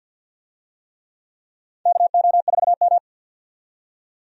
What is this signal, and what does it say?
Morse code, heard as a keyed tone, about 52 wpm, 700 Hz. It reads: KO4M
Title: USA Call Signs - 4 Characters - 50wpm